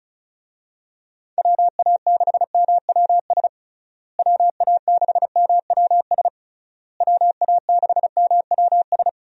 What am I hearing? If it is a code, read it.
Morse code, heard as a keyed tone, about 36 wpm, 700 Hz. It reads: WA6MWS WA6MWS WA6MWS
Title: USA Call Signs - 6 Characters - Repeated 3x - 35wpm